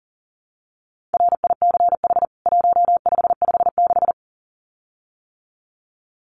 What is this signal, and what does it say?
Morse code, heard as a keyed tone, about 40 wpm, 700 Hz. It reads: RICH 1556
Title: CWT Exchange 40wpm